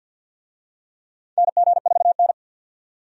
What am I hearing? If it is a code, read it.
Morse code, heard as a keyed tone, about 47 wpm, 700 Hz. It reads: NG4N
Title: USA Call Signs - 4 Characters - 50wpm